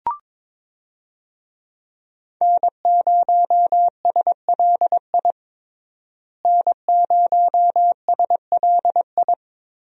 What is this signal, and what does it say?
Morse code, heard as a keyed tone, about 22 wpm, 700 Hz. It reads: N0SLI N0SLI
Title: USA Call Signs - 5 Characters - Repeated 2x - 22wpm